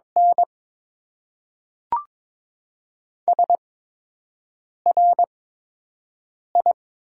Morse code, heard as a keyed tone, about 22 wpm, spaced as for 6 wpm, 700 Hz. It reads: N SRI
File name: Top 90 QSO Elements - 8x Character Spacing - 22wpm